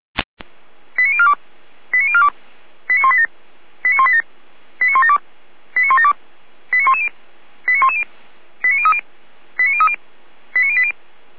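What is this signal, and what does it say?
Five-tone selective call (ZVEI), heard as quick runs of five tones, tones of about 70 ms. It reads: 89042 89042 89187 89187 89172 89172 89110 89110 89030 89030 89080
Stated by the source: ZVEI five-tone selective call 89099 received by SDR, from the Feuerwehr Orlamünde audio archive.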